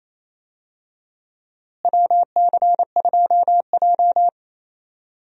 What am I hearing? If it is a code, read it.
Morse code, heard as a keyed tone, about 28 wpm, 700 Hz. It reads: WC2J
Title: USA Call Signs - 4 Characters - Repeated 3x - 28wpm